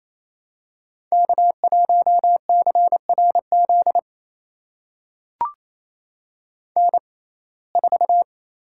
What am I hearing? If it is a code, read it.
Morse code, heard as a keyed tone, about 28 wpm, spaced as for 20 wpm, 700 Hz. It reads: K1CRZ N4
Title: USA Call Signs - 5 Characters - 6x Character Spacing - 28wpm